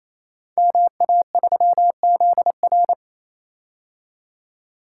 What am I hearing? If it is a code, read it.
Morse code, heard as a keyed tone, about 29 wpm, 700 Hz. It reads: MA3ZR
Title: Code Groups - 5 Letter:Numbers - 28wpm